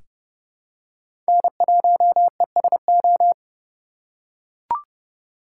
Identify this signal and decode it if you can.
Morse code, heard as a keyed tone, about 30 wpm, 700 Hz. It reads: N1ESO
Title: USA Call Signs - 5 Characters - 3x Character Spacing - 30wpm